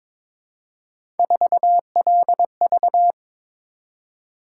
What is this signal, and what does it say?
Morse code, heard as a keyed tone, about 21 wpm, 700 Hz. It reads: 4LV
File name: practice / 22wpm / Three Letter-Number Combinations - 22wpm